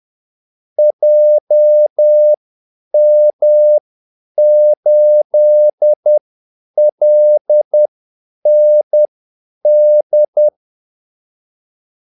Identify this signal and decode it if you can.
Morse code, heard as a keyed tone, about 10 wpm, spaced as for 8 wpm, 600 Hz. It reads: JM8LND
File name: Callsigns-10-8wpm